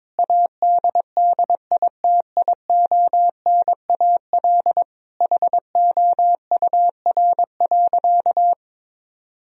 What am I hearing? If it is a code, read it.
Morse code, heard as a keyed tone, about 22 wpm, 700 Hz. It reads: ADDITIONAL HOUR.
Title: Sentences from Top 1000 Words - Review Part 4 - 22wpm